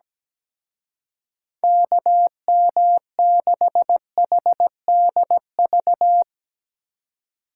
Morse code, heard as a keyed tone, about 18 wpm, 700 Hz. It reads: KM6HDV
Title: Speed Racing - USA 2x3 Call Signs 17wpm